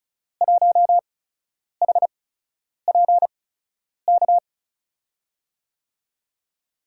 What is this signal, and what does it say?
Morse code, heard as a keyed tone, about 36 wpm, spaced as for 10 wpm, 700 Hz. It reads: 1HPK
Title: USA Call Signs - 6 Characters - 8x Character Spacing - 35wpm